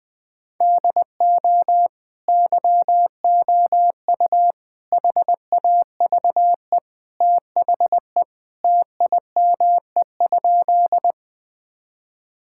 Morse code, heard as a keyed tone, about 21 wpm, 700 Hz. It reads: DO YOU HAVE THE TIME?
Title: Sentences with 2-Word Phrases - 20wpm